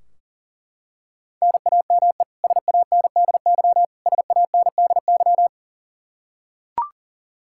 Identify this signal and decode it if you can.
Morse code, heard as a keyed tone, about 40 wpm, 700 Hz. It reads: NAME SANDY SANDY
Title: Practice QSO - Exchange 1 - Beginning - 40wpm